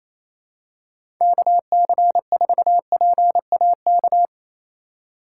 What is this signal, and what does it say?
Morse code, heard as a keyed tone, about 28 wpm, 700 Hz. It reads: KC4PAK
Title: USA Call Signs - 6 Characters - 4x Character Spacing - 28wpm